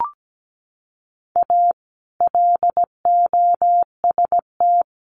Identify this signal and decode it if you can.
Morse code, heard as a keyed tone, about 17 wpm, 700 Hz. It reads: A LOST
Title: Sentences from Top 1400 Words 17wpm